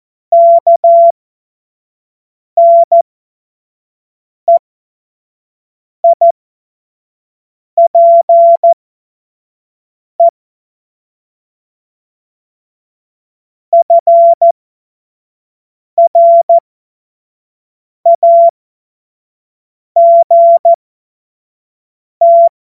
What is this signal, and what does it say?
Morse code, heard as a keyed tone, about 14 wpm, spaced as for 5 wpm, 680 Hz. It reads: KNEIPE FRAGT
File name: Morsekurs / Witze / Witz 11. Januar 2022, 25 F70